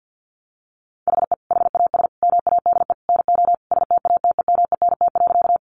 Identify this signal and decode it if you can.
Morse code, heard as a keyed tone, about 50 wpm, 700 Hz. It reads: HE HAS MADE NO STATEMENT.
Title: Sentences from Top 1200 Words 50wpm